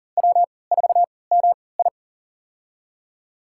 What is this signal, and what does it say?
Morse code, heard as a keyed tone, about 40 wpm, spaced as for 23 wpm, 700 Hz. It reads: W4MI
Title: USA Call Signs - 3x Character Spacing - 40wpm